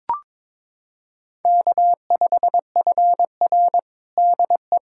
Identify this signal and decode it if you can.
Morse code, heard as a keyed tone, about 22 wpm, 700 Hz. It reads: K5FR DE
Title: USA Callsign DE Callsign - 4 Characters - 22wpm